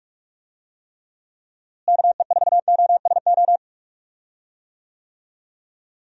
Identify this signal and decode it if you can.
Morse code, heard as a keyed tone, about 46 wpm, 700 Hz. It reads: KE4OSO
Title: USA Call Signs - 6 Characters - Repeated 3x - 45wpm